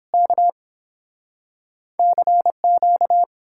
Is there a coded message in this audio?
Yes: Morse code from 0.1 to 3.2 s, heard as a keyed tone, about 31 wpm, 700 Hz. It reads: K CQ